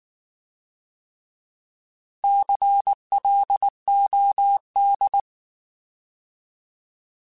Morse code, heard as a keyed tone, about 19 wpm, 800 Hz. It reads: CLOD